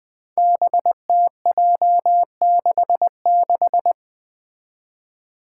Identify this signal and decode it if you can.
Morse code, heard as a keyed tone, about 20 wpm, 700 Hz. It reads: BTJ66
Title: Five Challenging Letter-Number Combinations 20wpm